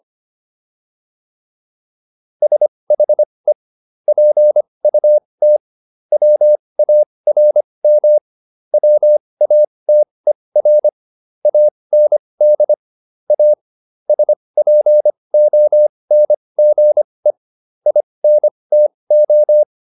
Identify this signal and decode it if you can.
Morse code, heard as a keyed tone, about 25 wpm, spaced as for 20 wpm, 600 Hz. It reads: SHE PUT WARM WATER AND A SPONGE INTO